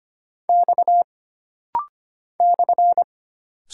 Morse code, heard as a keyed tone, about 25 wpm, 700 Hz. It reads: X /
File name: Single Character (Letter:Number:Punctuation:Prosign) - Rapid-Fire - 25wpm